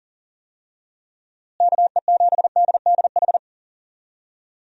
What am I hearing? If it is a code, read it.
Morse code, heard as a keyed tone, about 39 wpm, 700 Hz. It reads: KE7DDH
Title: USA Call Signs - 6 Characters - Repeated 3x - 40wpm